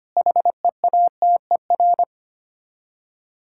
Morse code, heard as a keyed tone, about 25 wpm, 700 Hz. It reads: HEATER
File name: practice / 25wpm / Difficult Words with Hints - 25wpm